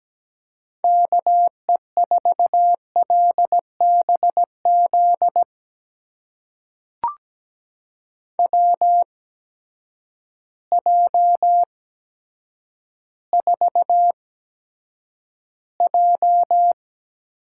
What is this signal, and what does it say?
Morse code, heard as a keyed tone, about 17 wpm, spaced as for 9 wpm, 700 Hz. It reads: KE4LBZ WJ4J
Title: USA Call Signs - 8x Character Spacing - 17wpm